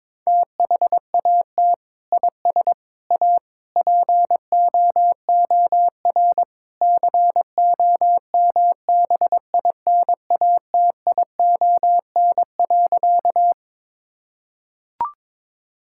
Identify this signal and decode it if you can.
Morse code, heard as a keyed tone, about 21 wpm, 700 Hz. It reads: THAT IS A POOR COMBINATION.
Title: Sentences from Top 1000 Words - Review Part 6 - 22wpm